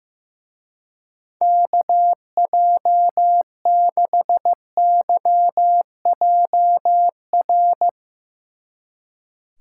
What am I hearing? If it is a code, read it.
Morse code, heard as a keyed tone, about 16 wpm, 700 Hz. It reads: KJ6YJR
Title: Speed Racing - USA 2x3 Call Signs 15wpm